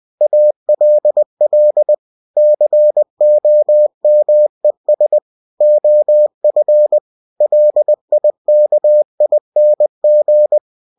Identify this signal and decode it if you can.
Morse code, heard as a keyed tone, about 20 wpm, 600 Hz. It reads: ALL COMES OF LIKING
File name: cw-books / winnie-the-pooh / winnie-20wpm-page-0004